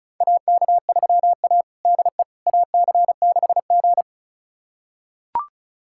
Morse code, heard as a keyed tone, about 36 wpm, 700 Hz. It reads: AK3A DE AC6G
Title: USA Callsign DE Callsign - 4 Characters - 35wpm